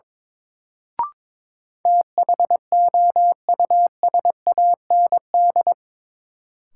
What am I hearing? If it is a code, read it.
Morse code, heard as a keyed tone, about 22 wpm, 700 Hz. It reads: THOUSAND
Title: Sentences from Top 600 Words - New Word Drill - 22wpm